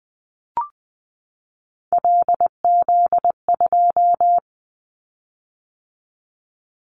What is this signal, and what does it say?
Morse code, heard as a keyed tone, about 20 wpm, 700 Hz. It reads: LZ2
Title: Callsign Prefixes for Top 50 DXCC - 20wpm